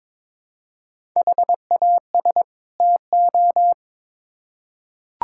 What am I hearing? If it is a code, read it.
Morse code, heard as a keyed tone, about 22 wpm, 700 Hz. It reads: HAS TO
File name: Build A Phrase - 2-Word Phrases - 22wpm